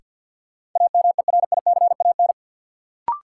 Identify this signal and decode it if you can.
Morse code, heard as a keyed tone, about 53 wpm, 700 Hz. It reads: AMERICAN
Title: Sets of 2 Words - New Word Drill - 50wpm